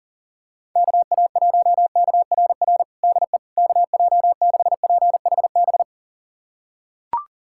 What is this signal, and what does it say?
Morse code, heard as a keyed tone, about 40 wpm, 700 Hz. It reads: KA1KRR DE KJ6PHB
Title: USA Callsign DE Callsign - 6 Characters - 40wpm